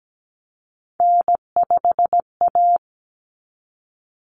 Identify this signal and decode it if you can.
Morse code, heard as a keyed tone, about 17 wpm, 690 Hz. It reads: N5A
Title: Common Cut Numbers - 3-Digit 17wpm